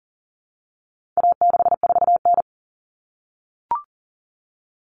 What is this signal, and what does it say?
Morse code, heard as a keyed tone, about 40 wpm, 700 Hz. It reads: A64N
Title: Common Cut Numbers - 4-Digit 40wpm